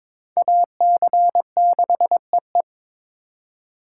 Morse code, heard as a keyed tone, about 22 wpm, 700 Hz. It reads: AC6EE
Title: USA Call Signs - 5 Characters - 2x Character Spacing - 22wpm